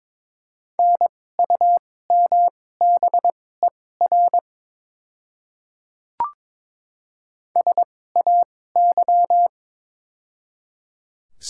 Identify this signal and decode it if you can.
Morse code, heard as a keyed tone, about 22 wpm, spaced as for 16 wpm, 700 Hz. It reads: NUMBER SAY
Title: 2x Character Spacing - Top 200 Words - 22wpm